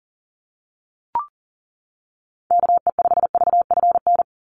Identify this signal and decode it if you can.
Morse code, heard as a keyed tone, about 40 wpm, 700 Hz. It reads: KE5VFN
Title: USA Call Signs 40wpm